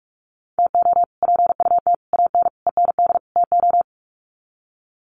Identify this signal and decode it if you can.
Morse code, heard as a keyed tone, about 44 wpm, 700 Hz. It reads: TO PUT AN END TO
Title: Sets of 5 Words Encore2 45wpm